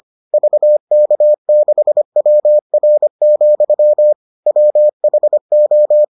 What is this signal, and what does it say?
Morse code, heard as a keyed tone, about 25 wpm, 600 Hz. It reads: VK6WR, WHO